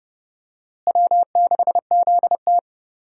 Morse code, heard as a keyed tone, about 30 wpm, 700 Hz. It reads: W6ZT
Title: USA Call Signs - 4 Characters - Repeated 2x - 30wpm